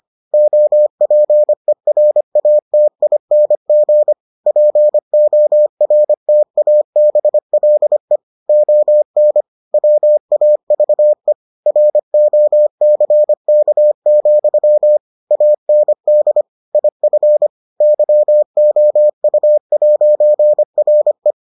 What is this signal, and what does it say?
Morse code, heard as a keyed tone, about 25 wpm, 600 Hz. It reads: OPERATING PORTABLE ON WAVE ROCK, AND IF YOU'RE